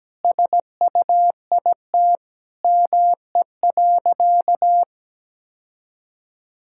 Morse code, heard as a keyed tone, about 17 wpm, 700 Hz. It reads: SUIT ME.
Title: Sentences from Top 1600 Words 17wpm